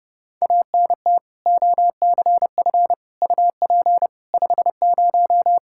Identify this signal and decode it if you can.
Morse code, heard as a keyed tone, about 30 wpm, 700 Hz. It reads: ANT OCF UP 50
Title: Practice QSO - Exchange 2 - Info - 30wpm